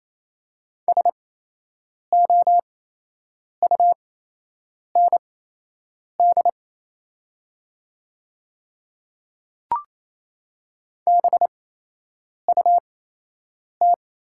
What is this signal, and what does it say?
Morse code, heard as a keyed tone, about 28 wpm, spaced as for 8 wpm, 700 Hz. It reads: SOUND BUT
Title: 8x Character Spacing - Top 100 Words - 28wpm